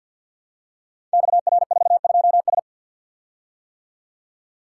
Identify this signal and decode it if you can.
Morse code, heard as a keyed tone, about 50 wpm, 700 Hz. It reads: XR42S
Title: Code Groups - 5 Letter:Numbers - 50wpm